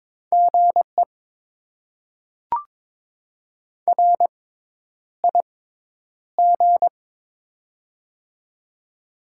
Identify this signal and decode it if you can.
Morse code, heard as a keyed tone, about 22 wpm, spaced as for 10 wpm, 690 Hz. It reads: GE RIG
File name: Top 20 QSO Elements - 6x Character Spacing - 22wpm